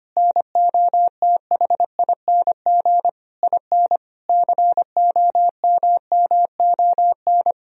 Morse code, heard as a keyed tone, about 25 wpm, 700 Hz. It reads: NOTHING IN COMMON
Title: Sentences from Top 1000 Words - Review Part 3 - 25wpm